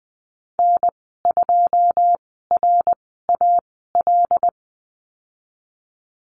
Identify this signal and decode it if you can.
Morse code, heard as a keyed tone, about 20 wpm, spaced as for 15 wpm, 700 Hz. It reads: N2RAL
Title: 088 Lesson 15g - Number 5 - Call Signs 20wpm